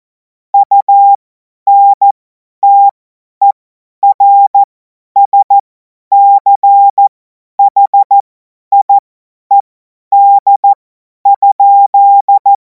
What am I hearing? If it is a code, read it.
Morse code, heard as a keyed tone, about 14 wpm, spaced as for 10 wpm, 800 Hz. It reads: UNTERSCHIED?
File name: Morsekurs / Witze / Witz 19. Mai 2026 50 F70